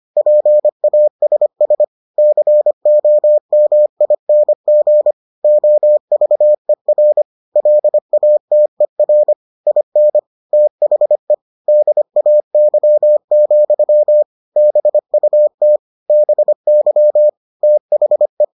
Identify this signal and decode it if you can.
Morse code, heard as a keyed tone, about 25 wpm, 600 Hz. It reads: PASS COMING OVER LATER IN THE DAY, BUT BY THE